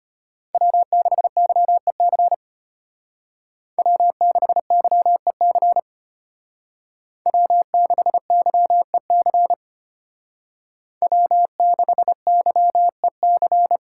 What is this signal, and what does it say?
Morse code, heard as a keyed tone, about 32 wpm, 700 Hz. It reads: W6YEC W6YEC W6YEC W6YEC